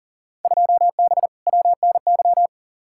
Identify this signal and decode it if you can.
Morse code, heard as a keyed tone, about 40 wpm, 700 Hz. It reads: 2B WNY
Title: Field Day Exchanges Encore - 40wpm